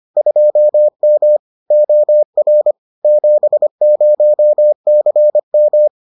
Morse code, heard as a keyed tone, about 25 wpm, 600 Hz. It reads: 2M OR 70CM